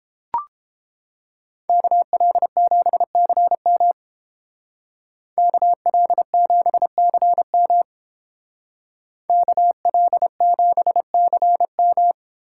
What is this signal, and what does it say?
Morse code, heard as a keyed tone, about 30 wpm, 700 Hz. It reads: KL7CM KL7CM KL7CM